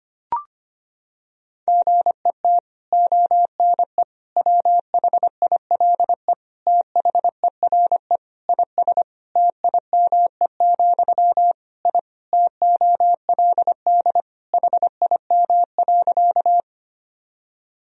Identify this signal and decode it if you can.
Morse code, heard as a keyed tone, about 25 wpm, 700 Hz. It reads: GET ONE WHILE THERE IS TIME, I TOLD HIM.
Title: Sentences from Top 1000 Words - Review Part 1 - 25wpm